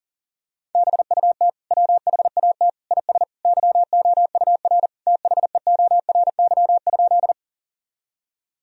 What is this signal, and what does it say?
Morse code, heard as a keyed tone, about 40 wpm, 700 Hz. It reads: BUT WHAT IS YOUR THEORY?
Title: Sentences from Top 1000 Words - Review Part 3 - 40wpm